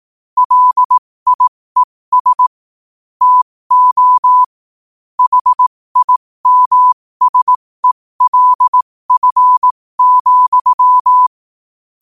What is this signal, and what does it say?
Morse code, heard as a keyed tone, about 18 wpm, spaced as for 15 wpm, 1000 Hz. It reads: LIES TO HIMSELF,
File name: qotdfiles / 15wpm